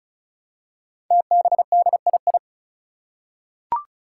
Morse code, heard as a keyed tone, about 36 wpm, 700 Hz. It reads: TBDII
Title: Five Challenging Letter-Number Combinations 35wpm